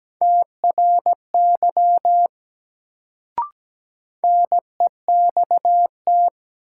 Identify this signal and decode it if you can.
Morse code, heard as a keyed tone, about 17 wpm, 700 Hz. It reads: TRY NEXT
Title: Sets of 2 Words - New Word Drill - 17wpm